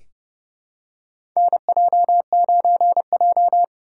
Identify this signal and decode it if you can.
Morse code, heard as a keyed tone, about 30 wpm, 700 Hz. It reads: NJ9J
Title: USA Callsign DE Callsign - 4 Characters - 30wpm